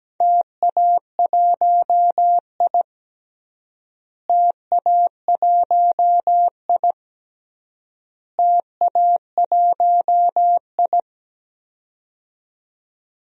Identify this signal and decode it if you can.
Morse code, heard as a keyed tone, about 17 wpm, 700 Hz. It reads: TA1I TA1I TA1I
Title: DXCC-100 Call Signs - 4 Characters - Repeated 3x - 17wpm